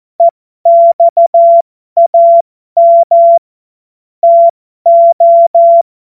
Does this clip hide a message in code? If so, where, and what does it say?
Yes: Morse code from 0.2 to 5.8 s, heard as a keyed tone, about 14 wpm, spaced as for 12 wpm, 680 Hz. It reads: EXAM TO